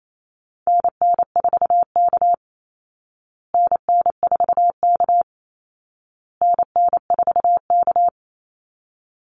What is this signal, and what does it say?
Morse code, heard as a keyed tone, about 28 wpm, 700 Hz. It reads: NN4K NN4K NN4K
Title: Call Signs Repeated 2x - 28wpm